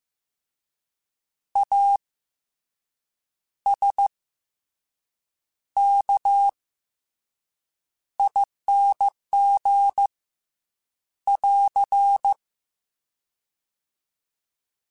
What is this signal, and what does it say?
Morse code, heard as a keyed tone, about 15 wpm, 800 Hz. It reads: A S K ING <AR>
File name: Special INGs-mp3s / Special INGs-mp3s / Fast ing @13